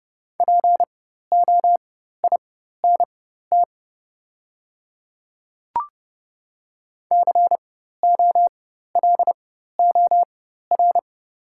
Morse code, heard as a keyed tone, about 30 wpm, spaced as for 14 wpm, 700 Hz. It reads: POINT COLOR
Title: 4x Character Spacing - Top 500 Words - 30wpm